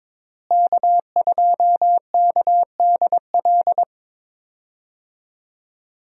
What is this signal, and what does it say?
Morse code, heard as a keyed tone, about 22 wpm, 700 Hz. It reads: K2KDL
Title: USA Call Signs - 5 Characters - Repeated 2x - 22wpm